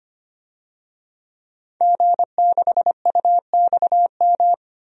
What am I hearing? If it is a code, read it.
Morse code, heard as a keyed tone, about 25 wpm, 700 Hz. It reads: G6UXM
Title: DXCC-100 Call Signs - 5 Characters - 25wpm